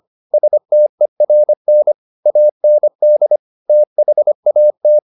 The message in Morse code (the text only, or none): STERN AND THAT